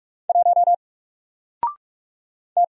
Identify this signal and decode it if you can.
Morse code, heard as a keyed tone, about 44 wpm, 700 Hz. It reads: 1 T
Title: Single Letter-Number 45wpm